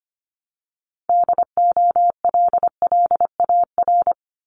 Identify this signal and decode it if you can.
Morse code, heard as a keyed tone, about 25 wpm, 700 Hz. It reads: DOLLAR
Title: Multi-letter Phonemes Repeated 3x 25wpm